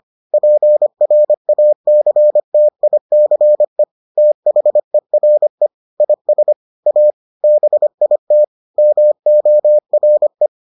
Morse code, heard as a keyed tone, about 25 wpm, 600 Hz. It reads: PRACTICE THERE IS A BIT MORE